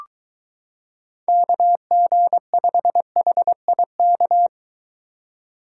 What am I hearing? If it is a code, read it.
Morse code, heard as a keyed tone, about 23 wpm, 700 Hz. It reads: KG5HIK